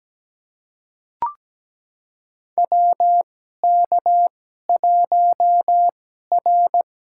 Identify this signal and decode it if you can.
Morse code, heard as a keyed tone, about 17 wpm, spaced as for 12 wpm, 700 Hz. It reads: WK1R
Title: USA Call Signs - 4 Characters - 2x Character Spacing - 17wpm